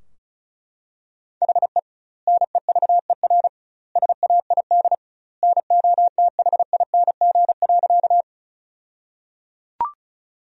Morse code, heard as a keyed tone, about 34 wpm, 700 Hz. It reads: HE NEVER SAID NOTHING.
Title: 2x Word-Spacing - Sentences from Top 500 Words - 35wpm